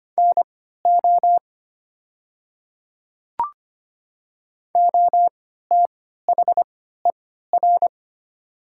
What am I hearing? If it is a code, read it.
Morse code, heard as a keyed tone, about 25 wpm, spaced as for 14 wpm, 700 Hz. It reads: NO OTHER
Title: 3x Character Spacing - Top 100 Words - 25wpm